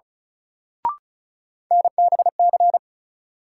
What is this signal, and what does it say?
Morse code, heard as a keyed tone, about 36 wpm, 700 Hz. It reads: NBC
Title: Sets of 5 Words Encore - New Word Drill - 35wpm